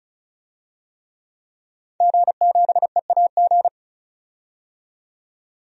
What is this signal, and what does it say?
Morse code, heard as a keyed tone, about 36 wpm, 700 Hz. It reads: G7EAG
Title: DXCC-100 Call Signs - 5 Characters - 35wpm